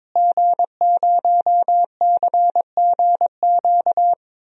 Morse code, heard as a keyed tone, about 22 wpm, 700 Hz. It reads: G0CGQ
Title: DXCC-100 Call Signs - 5 Characters - Repeated 3x - 22wpm